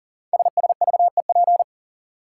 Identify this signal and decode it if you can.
Morse code, heard as a keyed tone, about 40 wpm, 700 Hz. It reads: SSVEP
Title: Code Groups - 5 Letters - 40wpm